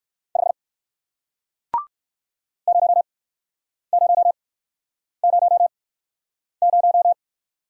Morse code, heard as a keyed tone, about 55 wpm, spaced as for 9 wpm, 700 Hz. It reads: 5 0000
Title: Speed-Racing Single Letter-Number 45wpm